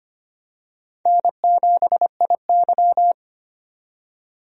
Morse code, heard as a keyed tone, about 25 wpm, 700 Hz. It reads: N7IY
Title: USA Call Signs - 4 Characters - 25wpm